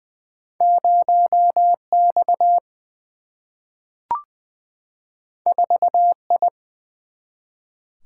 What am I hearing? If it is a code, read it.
Morse code, heard as a keyed tone, about 20 wpm, 700 Hz. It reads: 0 X 4 I
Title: Single-Letter Single-Number 20wpm